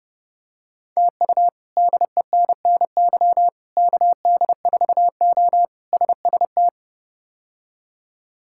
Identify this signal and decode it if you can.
Morse code, heard as a keyed tone, about 30 wpm, 700 Hz. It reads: TU DENNY KD4O SST
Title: K1USN SST QSO Practice - 30wpm